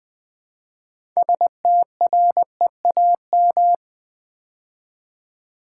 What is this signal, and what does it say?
Morse code, heard as a keyed tone, about 20 wpm, 700 Hz. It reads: STREAM
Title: Build A Word - 2 Letters At A Time - 6-Letter Words - 20wpm